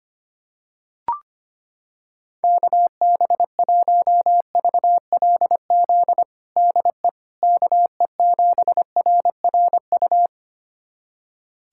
Morse code, heard as a keyed tone, about 25 wpm, 700 Hz. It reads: KB1VLZ DE KE7RRU
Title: USA Callsign DE Callsign 25wpm